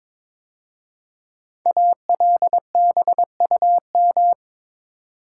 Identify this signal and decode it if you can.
Morse code, heard as a keyed tone, about 22 wpm, 700 Hz. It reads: ALBUM
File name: 5 Letter Words - First 3 Letters Given - 22wpm